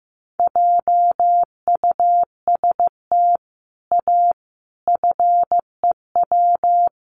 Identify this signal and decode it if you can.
Morse code, heard as a keyed tone, about 15 wpm, 700 Hz. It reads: JUST A FEW
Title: Sets of 5 Words Encore 15wpm